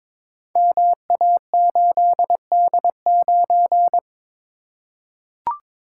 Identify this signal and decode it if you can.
Morse code, heard as a keyed tone, about 22 wpm, 700 Hz. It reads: MA8D9
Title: Code Groups - 5 Letter:Numbers - 22wpm